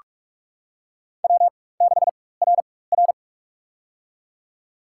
Morse code, heard as a keyed tone, about 46 wpm, spaced as for 21 wpm, 700 Hz. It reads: W6RR
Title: USA Call Signs - 4 Characters - 4x Character Spacing - 45wpm